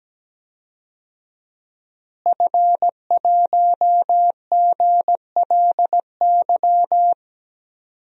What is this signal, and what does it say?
Morse code, heard as a keyed tone, about 17 wpm, 700 Hz. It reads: F1GLY
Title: DXCC-100 Call Signs - 5 Characters - Repeated 2x - 17wpm